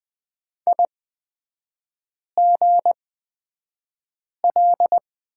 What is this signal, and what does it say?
Morse code, heard as a keyed tone, about 20 wpm, spaced as for 5 wpm, 700 Hz. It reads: IGL
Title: 236 Lesson 38e - Letter q - Three Characters 20wpm